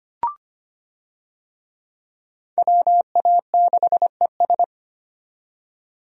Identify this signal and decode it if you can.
Morse code, heard as a keyed tone, about 25 wpm, 700 Hz. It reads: WA6ES
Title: USA Call Signs - 5 Characters - 25wpm